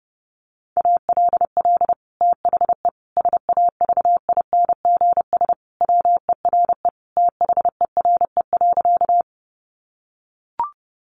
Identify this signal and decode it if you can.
Morse code, heard as a keyed tone, about 30 wpm, 700 Hz. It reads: ALL THE SAVINGS WERE THERE.
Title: Sentences from Top 1200 Words 30wpm